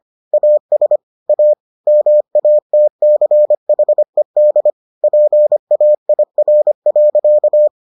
Morse code, heard as a keyed tone, about 25 wpm, 600 Hz. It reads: AS A MATCHED PAIR.